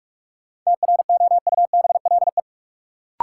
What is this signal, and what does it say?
Morse code, heard as a keyed tone, about 46 wpm, 700 Hz. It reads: TROUBLE
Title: Sentences from Top 1200 Words - New Word Drill Alt - 45wpm